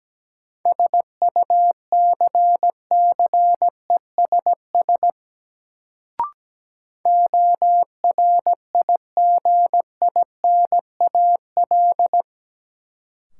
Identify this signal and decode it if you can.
Morse code, heard as a keyed tone, about 17 wpm, 700 Hz. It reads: SUCCESS ORIGINAL
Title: Sentences from Top 900 Words - New Word Drill - 17wpm